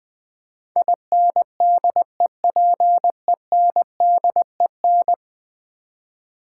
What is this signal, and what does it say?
Morse code, heard as a keyed tone, about 20 wpm, 700 Hz. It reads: INDEPENDEN
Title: Build A Word - 5 Letters At A Time - 12-Letter Words - 20wpm